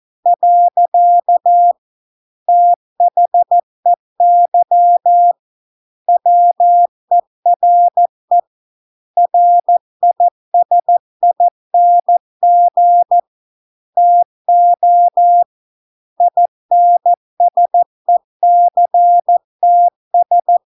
Morse code, heard as a keyed tone, about 14 wpm, 700 Hz. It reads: . THEY WERE RISING TO INSECTS